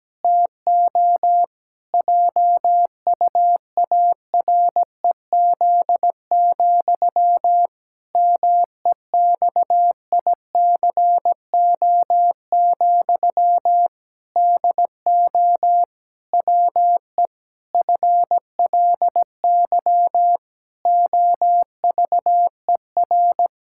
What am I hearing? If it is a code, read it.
Morse code, heard as a keyed tone, about 17 wpm, 700 Hz. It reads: TO JUAREZ, MEXICO, DO WE FLY OVER
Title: Pangram Challenge v2 - 17wpm